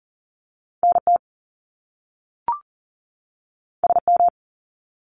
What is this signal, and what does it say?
Morse code, heard as a keyed tone, about 38 wpm, 700 Hz. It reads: NT SM
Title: Top Two Letter Combinations in Words 40wpm